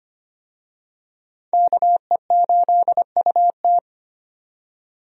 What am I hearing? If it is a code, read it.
Morse code, heard as a keyed tone, about 25 wpm, 700 Hz. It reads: KE8UT
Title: USA Call Signs - 5 Characters - 25wpm